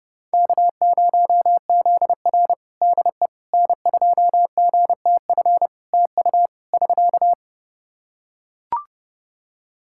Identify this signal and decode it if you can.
Morse code, heard as a keyed tone, about 30 wpm, 700 Hz. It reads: K0ZR DE N2GTF TU <SK>
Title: Practice QSO - Exchange 4 - Ending - 30wpm